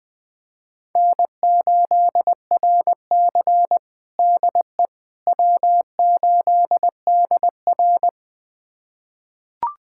Morse code, heard as a keyed tone, about 20 wpm, 700 Hz. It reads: N8RC DE W8DR
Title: USA Callsign DE Callsign - 4 Characters - 20wpm